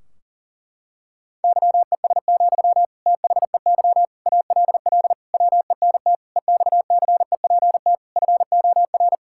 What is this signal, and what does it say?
Morse code, heard as a keyed tone, about 40 wpm, 700 Hz. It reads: YES, THEY ALL WENT EXCEPT FOR
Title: Sentences from Top 1600 Words 40wpm